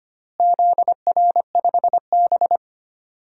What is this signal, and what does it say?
Morse code, heard as a keyed tone, about 25 wpm, 700 Hz. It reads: ZR5B
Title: DXCC-100 Call Signs - 4 Characters - Repeated 3x - 25wpm